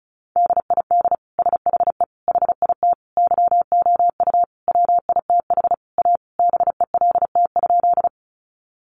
Morse code, heard as a keyed tone, about 34 wpm, 700 Hz. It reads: DID SHE HIT YOU WITH A BELT?
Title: Sentences from Top 1400 Words 35wpm